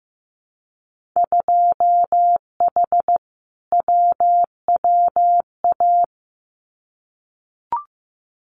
Morse code, heard as a keyed tone, about 15 wpm, 700 Hz. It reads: 2H WWA
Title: Winter Field Day Exchanges 15wpm